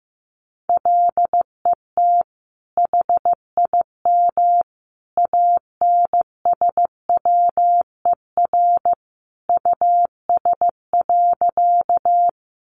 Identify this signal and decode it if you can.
Morse code, heard as a keyed tone, about 15 wpm, 700 Hz. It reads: LET HIM ANSWER US.
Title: Sentences from Top 300 Words 15wpm